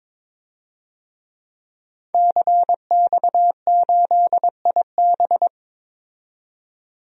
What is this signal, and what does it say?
Morse code, heard as a keyed tone, about 22 wpm, 700 Hz. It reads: CX8IB
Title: DXCC-100 Call Signs - 5 Characters - 22wpm